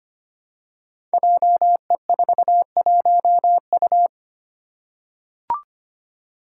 Morse code, heard as a keyed tone, about 25 wpm, 700 Hz. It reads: JE41U
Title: Five Challenging Letter-Number Combinations 25wpm